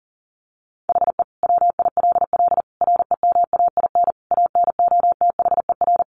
Morse code, heard as a keyed tone, about 40 wpm, 700 Hz. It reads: HE WILL REMAIN ANOTHER
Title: Sentences from Top 1200 Words 40wpm